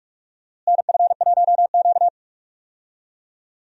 Morse code, heard as a keyed tone, about 44 wpm, 700 Hz. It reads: NF1Q
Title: USA Call Signs - 4 Characters - Repeated 3x - 45wpm